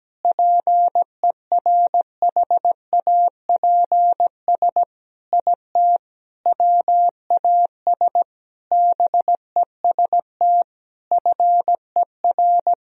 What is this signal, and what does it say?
Morse code, heard as a keyed tone, about 17 wpm, 700 Hz. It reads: PERHAPS IT WAS BEST FER
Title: Sentences from Top 500 Words and Abbreviations 17wpm